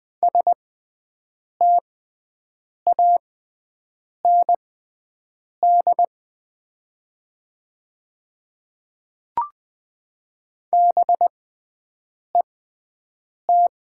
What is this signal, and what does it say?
Morse code, heard as a keyed tone, about 21 wpm, spaced as for 7 wpm, 700 Hz. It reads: STAND BET
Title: 6x Character Spacing - Top 300 Words - 20wpm